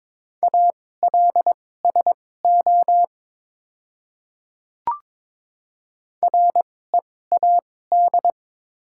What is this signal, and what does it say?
Morse code, heard as a keyed tone, about 22 wpm, spaced as for 16 wpm, 700 Hz. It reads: ALSO READ
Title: 2x Character Spacing - Top 200 Words - 22wpm